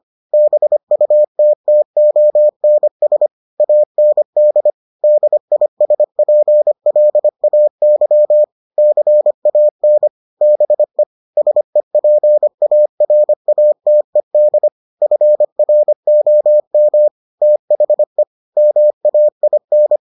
Morse code, heard as a keyed tone, about 25 wpm, 600 Hz. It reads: BUTTONS AND DISPLAY CAN BE SEPARATED FROM THE MAIN